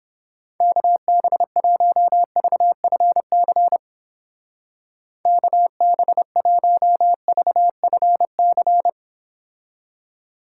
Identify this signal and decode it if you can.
Morse code, heard as a keyed tone, about 30 wpm, 700 Hz. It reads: KB1VFC KB1VFC